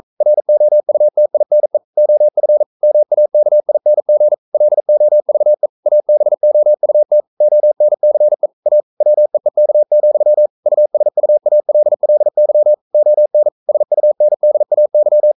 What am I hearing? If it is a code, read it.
Morse code, heard as a keyed tone, about 42 wpm, 600 Hz. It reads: ROUTINE OF MAKING LOVE ABOUT ONCE A WEEK, USUALLY ON SUNDAY
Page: Romanen Never af Ken Follet på engelsk.